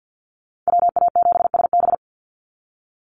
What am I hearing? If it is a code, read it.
Morse code, heard as a keyed tone, about 48 wpm, 700 Hz. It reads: WA7SB
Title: NA SOTA Chaser Callsigns 50wpm